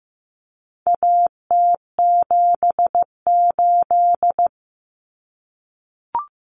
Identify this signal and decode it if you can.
Morse code, heard as a keyed tone, about 15 wpm, 700 Hz. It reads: AT78
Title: Common Cut Numbers - 4-Digit 15wpm